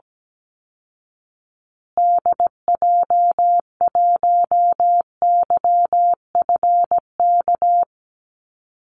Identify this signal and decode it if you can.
Morse code, heard as a keyed tone, about 17 wpm, 700 Hz. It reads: DJ1YFK
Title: Call Signs Repeated 2x - 17wpm